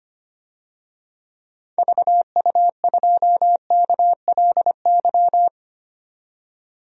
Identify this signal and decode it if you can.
Morse code, heard as a keyed tone, about 25 wpm, 700 Hz. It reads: VU2KLY
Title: DXCC-100 Call Signs - 6 Characters - 25wpm